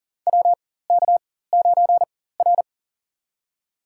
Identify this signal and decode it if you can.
Morse code, heard as a keyed tone, about 40 wpm, spaced as for 19 wpm, 700 Hz. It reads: WK9R
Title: USA Call Signs - 4 Characters - 4x Character Spacing - 40wpm